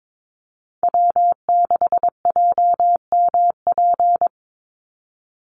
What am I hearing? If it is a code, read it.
Morse code, heard as a keyed tone, about 22 wpm, 700 Hz. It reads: W6JMP
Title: NA SOTA Chaser Callsigns 22wpm